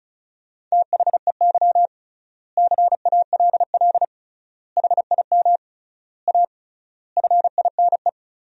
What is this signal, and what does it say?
Morse code, heard as a keyed tone, about 34 wpm, 700 Hz. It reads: THEY CALL HIM A FINE
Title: 3x Word-Spacing - Sentences from Top 500 Words - 35wpm